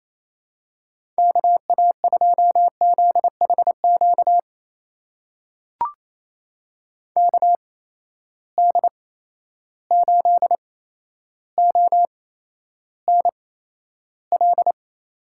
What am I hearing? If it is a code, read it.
Morse code, heard as a keyed tone, about 28 wpm, spaced as for 12 wpm, 700 Hz. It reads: KA2ZHQ KD8ONL
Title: USA Call Signs - 6 Characters - 8x Character Spacing - 28wpm